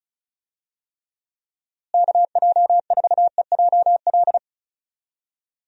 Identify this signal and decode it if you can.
Morse code, heard as a keyed tone, about 34 wpm, 700 Hz. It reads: KJ4EJL
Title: USA Call Signs - 6 Characters - Repeated 2x - 35wpm